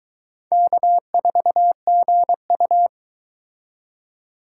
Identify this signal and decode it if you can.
Morse code, heard as a keyed tone, about 23 wpm, 700 Hz. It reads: K4GU